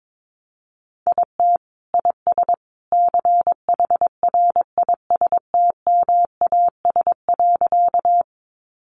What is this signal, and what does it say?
Morse code, heard as a keyed tone, about 22 wpm, 700 Hz. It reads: IT IS CHRISTMAS.
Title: Easy Phrases 22wpm